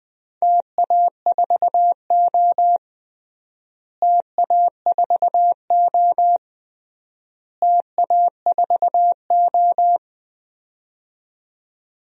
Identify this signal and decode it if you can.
Morse code, heard as a keyed tone, about 20 wpm, 700 Hz. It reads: TA4O TA4O TA4O
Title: DXCC-100 Call Signs - 4 Characters - Repeated 3x - 20wpm